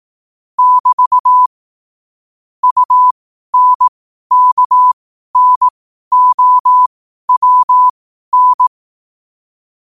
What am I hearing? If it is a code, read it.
Morse code, heard as a keyed tone, about 18 wpm, spaced as for 12 wpm, 1000 Hz. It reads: <BT> UNKNOWN
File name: qotdfiles / 12wpm